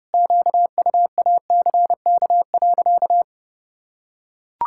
Morse code, heard as a keyed tone, about 30 wpm, 700 Hz. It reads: QUACK.
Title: Pangram Challenge v2 - 30wpm